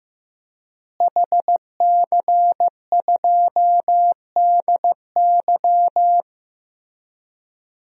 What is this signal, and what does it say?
Morse code, heard as a keyed tone, about 15 wpm, 700 Hz. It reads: HC2DY
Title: DXCC-100 Call Signs - 5 Characters - Repeated 3x - 15wpm